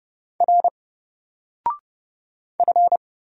Morse code, heard as a keyed tone, about 30 wpm, 700 Hz. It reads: R F
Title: Single Character (Letter:Number:Punctuation:Prosign) 30wpm